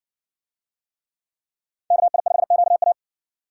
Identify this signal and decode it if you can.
Morse code, heard as a keyed tone, about 53 wpm, 700 Hz. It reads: KI5QA